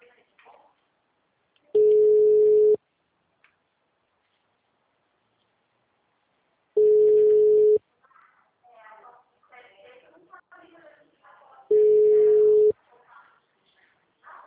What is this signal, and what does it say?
Ringback tone (Europe, 425 Hz), with speech.